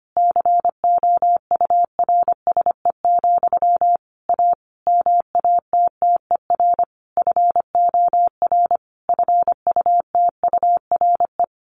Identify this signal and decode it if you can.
Morse code, heard as a keyed tone, about 25 wpm, 700 Hz. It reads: COURSE, A MATTER FOR FUTURE
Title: Sentences from Top 1500 Words 25wpm